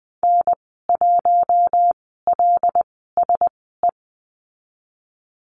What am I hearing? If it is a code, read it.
Morse code, heard as a keyed tone, about 20 wpm, spaced as for 15 wpm, 700 Hz. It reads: N1LSE
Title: 076 Lesson 13g - Letter L - Call Signs 20wpm